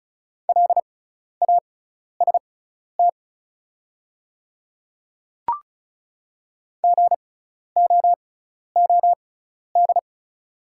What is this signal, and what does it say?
Morse code, heard as a keyed tone, about 34 wpm, spaced as for 12 wpm, 700 Hz. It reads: LAST GOOD
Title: 6x Character Spacing - Top 300 Words - 35wpm